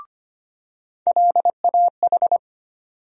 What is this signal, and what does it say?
Morse code, heard as a keyed tone, about 25 wpm, 700 Hz. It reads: LAH
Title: USA Call Sign Postfixes - 3 Letters 25wpm